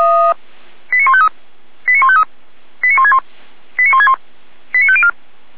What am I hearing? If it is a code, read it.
Five-tone selective call (ZVEI), heard as quick runs of five tones, tones of about 70 ms. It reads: 89153 89153 89161 89161 89504
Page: SDR/ZVEI